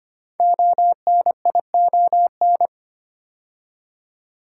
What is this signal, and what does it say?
Morse code, heard as a keyed tone, about 25 wpm, 700 Hz. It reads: ONION
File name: 5 Letter Words - First 3 Letters Given - 25wpm